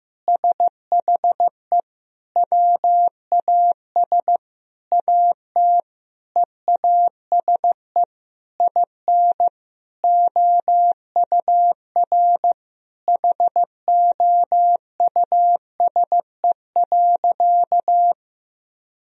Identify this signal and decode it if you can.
Morse code, heard as a keyed tone, about 15 wpm, 700 Hz. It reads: SHE WAS AT EASE IN OUR HOUSE.
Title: Sentences from Top 1000 Words - Review Part 1 - 15wpm